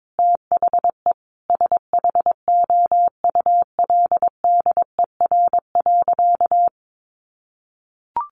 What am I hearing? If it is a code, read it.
Morse code, heard as a keyed tone, about 22 wpm, 700 Hz. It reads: THE SHOULDER.
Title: Additional Easy Phrases 22wpm